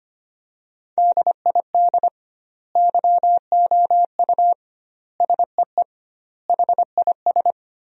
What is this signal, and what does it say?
Morse code, heard as a keyed tone, about 25 wpm, 700 Hz. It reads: DID YOU SEE HIS
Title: 2x Word-Spacing - Sentences from Top 300 Words - 25wpm